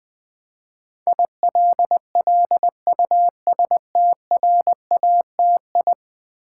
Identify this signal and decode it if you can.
Morse code, heard as a keyed tone, about 20 wpm, 700 Hz. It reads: ILLUSTRATI
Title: Build A Word - 5 Letters At A Time - 12-Letter Words - 20wpm